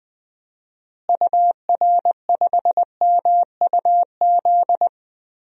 Morse code, heard as a keyed tone, about 20 wpm, 700 Hz. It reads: UR5MUZ
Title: DXCC-100 Call Signs - 6 Characters - Repeated 2x - 20wpm